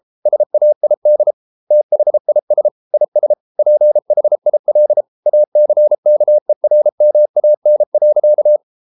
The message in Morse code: SAID THIS IS PHIL ACKERMAN.

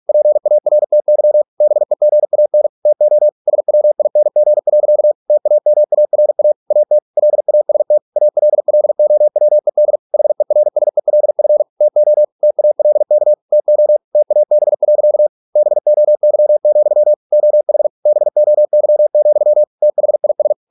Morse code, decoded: PARTY BEGAN TO SWING. TAMARA AT LAST ALLOWED HERSELF TO TALK TO TAB. BOY, OH BOY, THIS

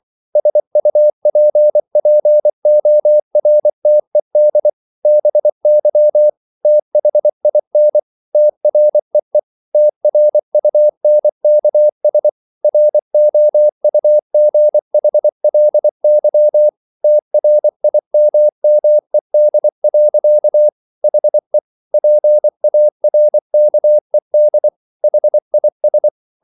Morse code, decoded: SUPPORTED BY THIN TREE TRUNKS ROUGHLY TRIMMED. HE PARKED HIS

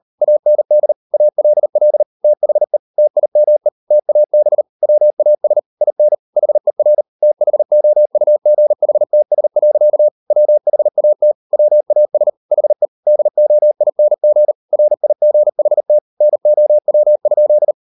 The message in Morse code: AND ALL THE TIME TAB WAS IN HER THOUGHTS. WHAT WAS HE DOING RIGHT NOW?

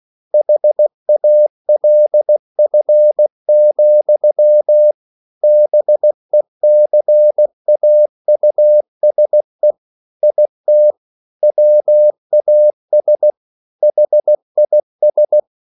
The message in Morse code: HALF, BECAUSE IT WAS HIS

